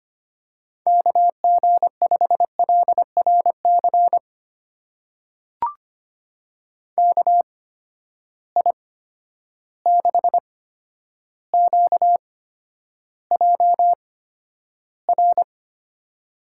Morse code, heard as a keyed tone, about 25 wpm, spaced as for 11 wpm, 700 Hz. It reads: KG5LRC KI6QJR
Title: USA Call Signs - 6 Characters - 8x Character Spacing - 25wpm